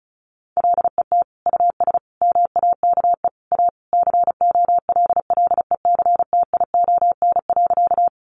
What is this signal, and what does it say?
Morse code, heard as a keyed tone, about 34 wpm, 700 Hz. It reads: LET US MAKE A COLLECTION.
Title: Sentences from Top 1200 Words 35wpm